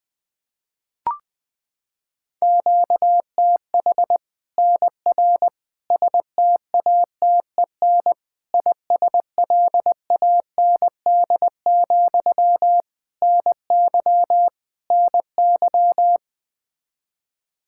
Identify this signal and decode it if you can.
Morse code, heard as a keyed tone, about 20 wpm, 700 Hz. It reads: QTH NR STATEN ISLAND, NY NY
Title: Practice QSO - Exchange 1 - Beginning - 20wpm